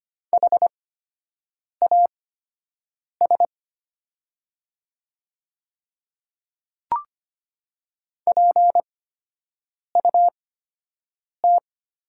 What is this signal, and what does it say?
Morse code, heard as a keyed tone, about 25 wpm, spaced as for 7 wpm, 700 Hz. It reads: HAS PUT
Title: 8x Character Spacing - Top 200 Words - 25wpm